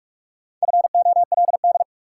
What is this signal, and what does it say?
Morse code, heard as a keyed tone, about 43 wpm, 700 Hz. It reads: FOLD
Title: Four Letter Words - 45wpm